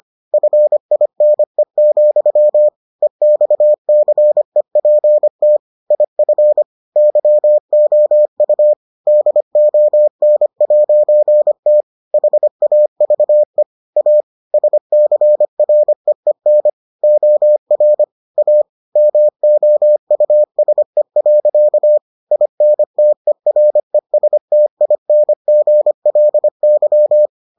Morse code, heard as a keyed tone, about 25 wpm, 600 Hz. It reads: FINE, EXCEPT IF YOU DON'T HAVE A SCREEN OR A MOUSE. INTERESTINGLY